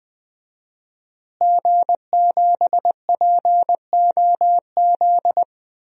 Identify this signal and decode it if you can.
Morse code, heard as a keyed tone, about 20 wpm, 700 Hz. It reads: G7POZ
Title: DXCC-100 Call Signs - 5 Characters - Repeated 3x - 20wpm